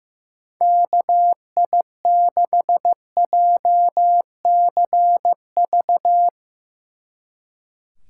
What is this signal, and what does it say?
Morse code, heard as a keyed tone, about 15 wpm, 700 Hz. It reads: KI6JCV
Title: Speed Racing - USA 2x3 Call Signs 15wpm